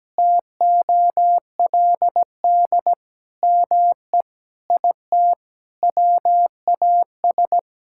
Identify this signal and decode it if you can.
Morse code, heard as a keyed tone, about 17 wpm, 700 Hz. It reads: TOLD ME IT WAS
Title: Sentences from Top 1600 Words 17wpm